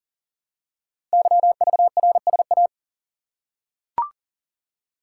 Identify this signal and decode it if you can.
Morse code, heard as a keyed tone, about 38 wpm, 700 Hz. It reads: YVRSA
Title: Code Groups - 5 Letters - 40wpm